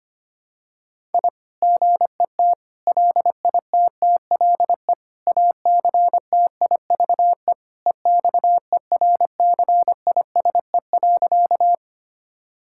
Morse code, heard as a keyed tone, about 25 wpm, 700 Hz. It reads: I GET LITTLE ACTIVE EXERCISE.
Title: Sentences from Top 1000 Words - Review Part 6 - 25wpm